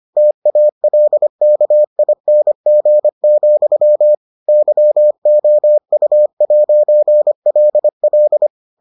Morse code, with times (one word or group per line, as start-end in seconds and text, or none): 0.2-4.1 TALKING,
4.5-8.5 YOU'LL